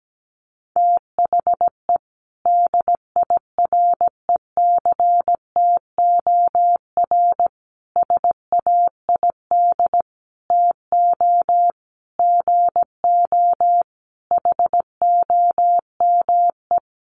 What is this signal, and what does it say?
Morse code, heard as a keyed tone, about 17 wpm, 700 Hz. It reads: THE DIRECTOR SAID TO GO HOME
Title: Sentences from Top 1300 Words 17wpm